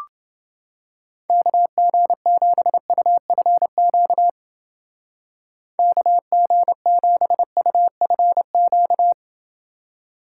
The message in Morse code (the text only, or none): KG7UFQ KG7UFQ